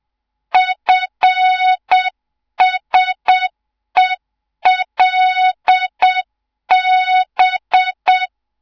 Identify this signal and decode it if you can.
Morse code, heard as a keyed tone, about 7 wpm, 3700 Hz, with noise. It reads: FSELB
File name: Morze-HA2MN / hang / 24 GYAKORLAT